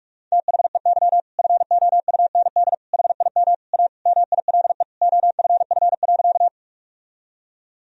Morse code, heard as a keyed tone, about 46 wpm, 700 Hz. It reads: THEY FOUND HIM A MILE OFF.
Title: Sentences from Top 1000 Words - Review Part 1 - 45wpm